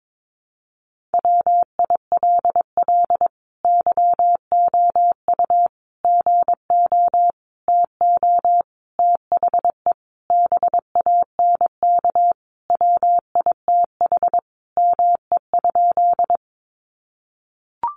Morse code, heard as a keyed tone, about 22 wpm, 700 Hz. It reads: WILL YOU GO TO THE BANK WITH ME?
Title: Sentences from Top 900 Words 22wpm